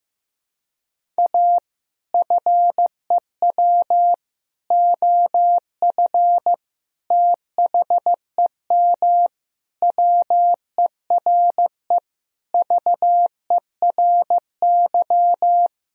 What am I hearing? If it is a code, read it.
Morse code, heard as a keyed tone, about 15 wpm, 700 Hz. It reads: A FEW OF THEM WERE VERY
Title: Sentences from Top 1000 Words - Review Part 1 - 15wpm